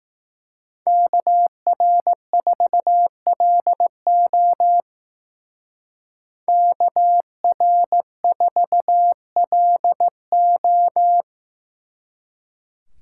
Morse code, heard as a keyed tone, about 18 wpm, 700 Hz. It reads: KR4LO KR4LO
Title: Speed Racing - USA 2x2 Call Signs 15wpm